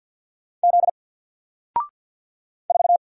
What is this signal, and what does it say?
Morse code, heard as a keyed tone, about 50 wpm, 700 Hz. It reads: Z 4
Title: Single Letter-Number 50wpm